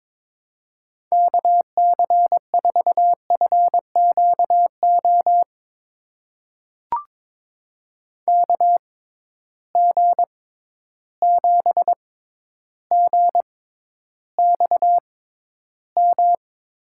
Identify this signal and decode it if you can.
Morse code, heard as a keyed tone, about 23 wpm, spaced as for 11 wpm, 700 Hz. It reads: KC4FQO KG7GXM
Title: USA Call Signs - 6 Characters - 6x Character Spacing - 22wpm